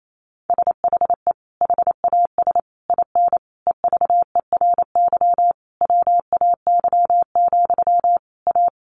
Morse code, heard as a keyed tone, about 28 wpm, 700 Hz. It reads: SHE HAS IN EVERY WAY, A
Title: Sentences from Top 1300 Words 28wpm